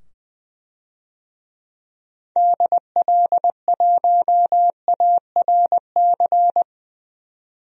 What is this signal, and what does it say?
Morse code, heard as a keyed tone, about 21 wpm, 700 Hz. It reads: DL1ARC
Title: DXCC-100 Call Signs - 6 Characters - 20wpm